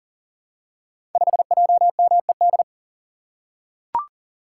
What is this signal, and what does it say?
Morse code, heard as a keyed tone, about 38 wpm, 700 Hz. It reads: 5JMED
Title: Code Groups - 5 Letter:Numbers - 40wpm